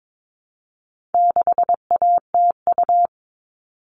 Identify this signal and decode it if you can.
Morse code, heard as a keyed tone, about 22 wpm, 700 Hz. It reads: 6ATU
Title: All Cut Numbers - 4-Digit 22wpm